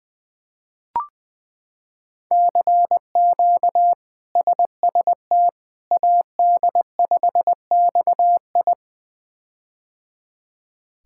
Morse code, heard as a keyed tone, about 20 wpm, 700 Hz. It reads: CQ SST AD5XI
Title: K1USN SST QSO Practice - 20wpm